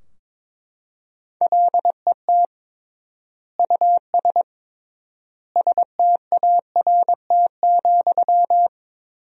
Morse code, heard as a keyed tone, about 22 wpm, 700 Hz. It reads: LET US START,
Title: 3x Word-Spacing - Sentences from Top 300 Words - 22wpm